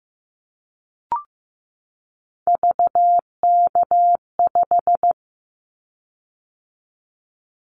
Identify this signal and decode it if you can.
Morse code, heard as a keyed tone, about 15 wpm, 700 Hz. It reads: VK5
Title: Callsign Prefixes for Top 50 DXCC - 15wpm